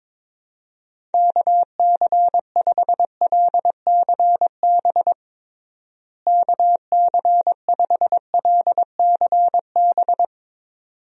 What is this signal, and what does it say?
Morse code, heard as a keyed tone, about 22 wpm, 700 Hz. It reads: KC5LCB KC5LCB
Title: USA Call Signs - 6 Characters - Repeated 2x - 22wpm